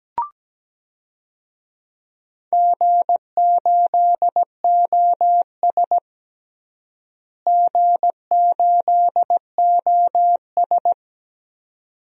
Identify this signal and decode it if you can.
Morse code, heard as a keyed tone, about 17 wpm, 700 Hz. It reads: G8OS G8OS
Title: DXCC-100 Call Signs - 4 Characters - Repeated 2x - 17wpm